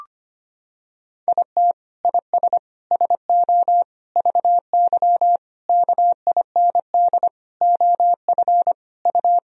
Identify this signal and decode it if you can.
Morse code, heard as a keyed tone, about 25 wpm, 700 Hz. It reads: IT IS SO VY KIND OF U
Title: Sentences from Top 300 Words and Abbreviations 25wpm